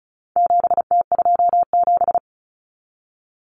Morse code, heard as a keyed tone, about 34 wpm, 700 Hz. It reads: 7T27
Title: Common Cut Numbers - 4-Digit 35wpm